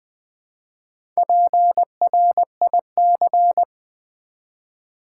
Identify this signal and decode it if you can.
Morse code, heard as a keyed tone, about 20 wpm, 700 Hz. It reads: PRIC
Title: Build A Word - 1 Letter At A Time - 5-Letter Words - 20wpm